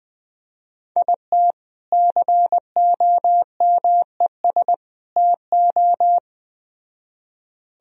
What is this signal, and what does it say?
Morse code, heard as a keyed tone, about 19 wpm, 700 Hz. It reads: IT COMES TO
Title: Speed-Racing - Sets of 3 Words 17wpm